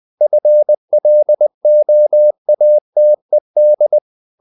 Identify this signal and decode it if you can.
Morse code, heard as a keyed tone, about 20 wpm, 600 Hz. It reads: FLOATED
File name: the old man and the sea part 2 20wpm